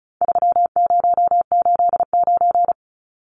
Morse code, heard as a keyed tone, about 34 wpm, 700 Hz. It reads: 3089
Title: Numbers - 4-Digit 35wpm